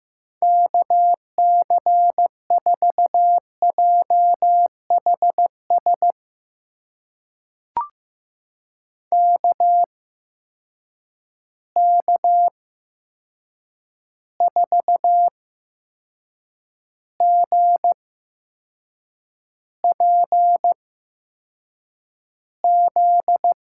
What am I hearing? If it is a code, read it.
Morse code, heard as a keyed tone, about 15 wpm, spaced as for 6 wpm, 700 Hz. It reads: KC4JHS KK4GPZ
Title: USA Call Signs - 6 Characters - 8x Character Spacing - 15wpm